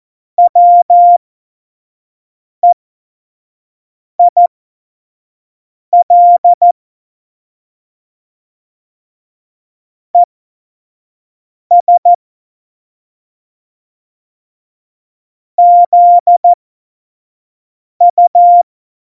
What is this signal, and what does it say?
Morse code, heard as a keyed tone, about 14 wpm, spaced as for 5 wpm, 700 Hz. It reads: WEIL ES ZU
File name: Morsekurs / Witze / Witz 22. August 17, Tempo 25 F70